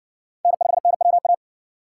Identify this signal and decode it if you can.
Morse code, heard as a keyed tone, about 60 wpm, 700 Hz. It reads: N5NWA